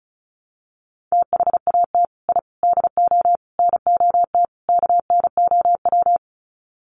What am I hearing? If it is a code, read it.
Morse code, heard as a keyed tone, about 34 wpm, 700 Hz. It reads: THAT I DO NOT KNOW
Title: Sets of 5 Words Encore2 35wpm